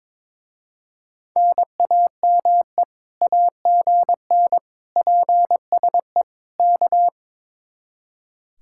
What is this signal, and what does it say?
Morse code, heard as a keyed tone, about 22 wpm, 700 Hz. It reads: NAME AGN PSE K
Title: Speed-Racing Common QSO Phrases - 22wpm